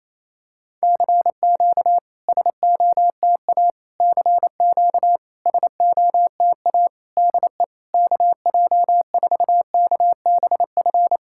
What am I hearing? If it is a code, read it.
Morse code, heard as a keyed tone, about 28 wpm, 700 Hz. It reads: CQ SOTA CQ SOTA DE KJ4KBF
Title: SOTA QSO Practice 28wpm